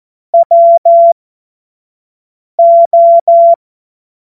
Morse code, heard as a keyed tone, about 14 wpm, spaced as for 5 wpm, 680 Hz. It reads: WO
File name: Morsekurs / Witze / Witz 14. März 2023 25 F70